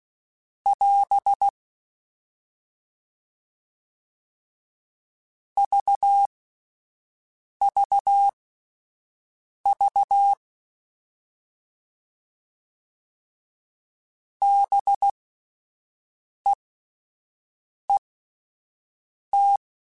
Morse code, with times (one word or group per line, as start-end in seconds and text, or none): 0.7-1.5 <AS>
5.6-10.3 VVV
14.4-19.6 BEET